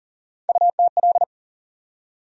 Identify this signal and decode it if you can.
Morse code, heard as a keyed tone, about 40 wpm, 700 Hz. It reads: UTL
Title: USA Call Sign Postfixes - 3 Letters 40wpm